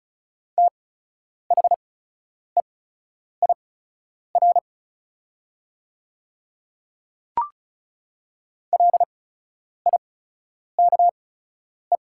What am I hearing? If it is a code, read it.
Morse code, heard as a keyed tone, about 34 wpm, spaced as for 10 wpm, 690 Hz. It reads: THEIR LIKE
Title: 8x Character Spacing - Top 100 Words - 35wpm